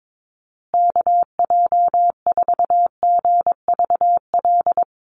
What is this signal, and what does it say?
Morse code, heard as a keyed tone, about 22 wpm, 700 Hz. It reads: KJ4GVL
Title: USA Call Signs Repeated 3x 22wpm